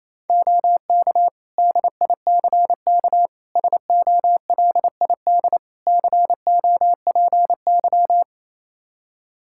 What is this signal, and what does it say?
Morse code, heard as a keyed tone, about 28 wpm, 700 Hz. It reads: OK DICK SOLID COPY
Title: Practice QSO - Exchange 3 - More Info - 28wpm